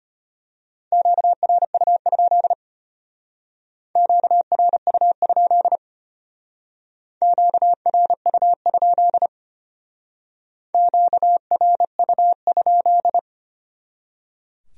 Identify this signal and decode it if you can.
Morse code, heard as a keyed tone, about 32 wpm, 700 Hz. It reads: QRU? QRU? QRU? QRU?